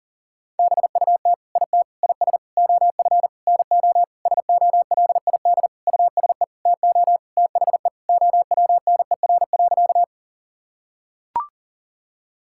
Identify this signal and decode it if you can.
Morse code, heard as a keyed tone, about 40 wpm, 700 Hz. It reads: BUT IT IS OF NO SOLID USE TO THE OWNER.
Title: Sentences from Top 1600 Words 40wpm